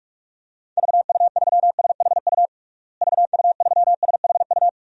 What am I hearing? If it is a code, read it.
Morse code, heard as a keyed tone, about 46 wpm, 700 Hz. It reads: VU3SHU VU3SHU
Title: DXCC-100 Call Signs - 6 Characters - Repeated 2x - 45wpm